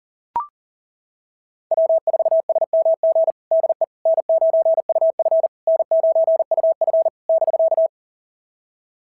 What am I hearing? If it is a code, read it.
Morse code, heard as a keyed tone, about 38 wpm, 690 Hz. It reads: W4SMG DE N9UF N9UF <BK>
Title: SOTA QSO Practice 40wpm